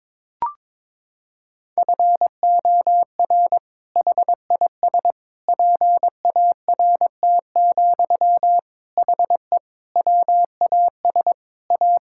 Morse code, heard as a keyed tone, about 22 wpm, 700 Hz. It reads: FOR HIS PART, HE WAS A
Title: Sentences from Top 1000 Words - Review Part 5 - 22wpm